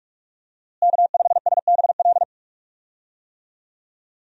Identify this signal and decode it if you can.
Morse code, heard as a keyed tone, about 46 wpm, 700 Hz. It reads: K5SBL
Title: USA Call Signs - 5 Characters - Repeated 2x - 45wpm